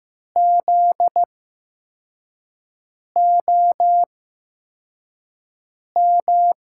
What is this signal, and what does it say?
Morse code, heard as a keyed tone, about 15 wpm, spaced as for 4 wpm, 700 Hz. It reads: ZOM
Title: USA Call Signs - 6 Characters - 8x Character Spacing - 15wpm